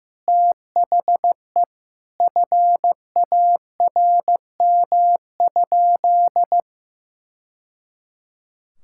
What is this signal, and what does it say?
Morse code, heard as a keyed tone, about 16 wpm, 700 Hz. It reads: THE FARM?
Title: Sentences from Top 1000 Words - Review Part 5 - 15wpm